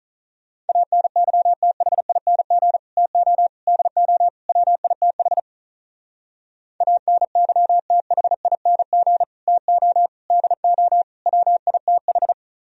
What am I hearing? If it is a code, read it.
Morse code, heard as a keyed tone, about 45 wpm, 700 Hz. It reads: ANYTHING TO DO WITH ANYTHING TO DO WITH